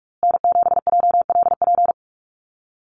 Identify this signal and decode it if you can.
Morse code, heard as a keyed tone, about 44 wpm, 700 Hz. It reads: N7JLP
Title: USA Call Signs Repeated 2x 45wpm